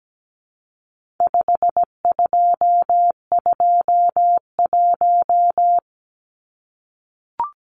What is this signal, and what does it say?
Morse code, heard as a keyed tone, about 17 wpm, 700 Hz. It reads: 5221
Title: Numbers - 4-Digit 17wpm